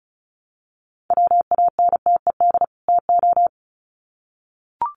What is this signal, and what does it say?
Morse code, heard as a keyed tone, about 34 wpm, 700 Hz. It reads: WANTED TO
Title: Sets of 2 Words 35wpm